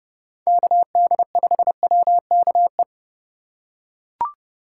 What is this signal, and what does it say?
Morse code, heard as a keyed tone, about 30 wpm, 700 Hz. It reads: KD5WKE
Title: USA Call Signs - 6 Characters - 2x Character Spacing - 30wpm